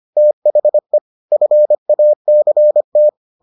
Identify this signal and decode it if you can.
Morse code, heard as a keyed tone, about 25 wpm, 600 Hz. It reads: THE FACT